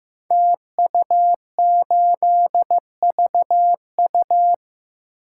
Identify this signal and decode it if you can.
Morse code, heard as a keyed tone, about 15 wpm, 700 Hz. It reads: TU8VU
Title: Code Groups - 5 Letter:Numbers - 15wpm